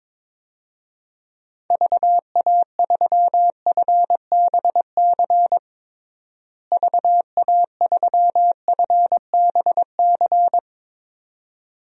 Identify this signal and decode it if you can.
Morse code, heard as a keyed tone, about 22 wpm, 700 Hz. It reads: VA3FBC VA3FBC
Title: DXCC-100 Call Signs - 6 Characters - Repeated 2x - 22wpm